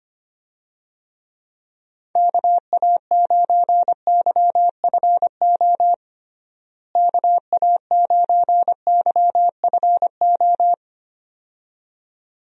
Morse code, heard as a keyed tone, about 25 wpm, 700 Hz. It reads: KA9YFO KA9YFO
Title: USA Call Signs - 6 Characters - Repeated 2x - 25wpm